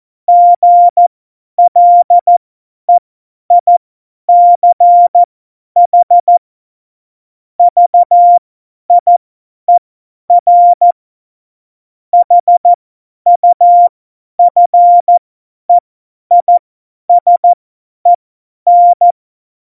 Morse code, heard as a keyed tone, about 14 wpm, spaced as for 10 wpm, 700 Hz. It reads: GLEICH VIER HUFEISEN
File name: Morsekurs / Witze / Witz 05 Mai 2020, 50 F70